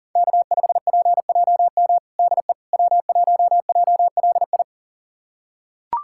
Morse code, heard as a keyed tone, about 39 wpm, 700 Hz. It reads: K5PJM DE W1JLI
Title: USA Callsign DE Callsign - 5 Characters - 40wpm